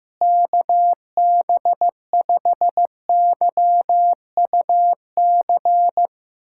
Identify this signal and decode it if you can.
Morse code, heard as a keyed tone, about 15 wpm, 700 Hz. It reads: KB5YUC
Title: USA Callsign DE Callsign 15wpm